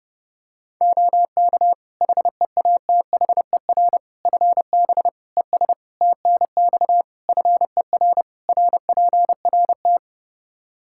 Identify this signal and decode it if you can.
Morse code, heard as a keyed tone, about 30 wpm, 700 Hz. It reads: OK HEATHER FB ES TNX FER RPRT
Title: K1USN SST QSO Practice 30wpm